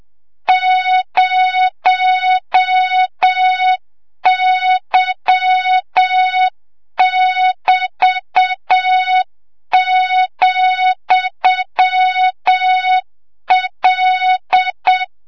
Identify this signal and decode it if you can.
Morse code, heard as a keyed tone, about 7 wpm, 3700 Hz, with noise. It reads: 0Y<BT>,L
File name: Morze-HA2MN / hang / 25 GYAKORLAT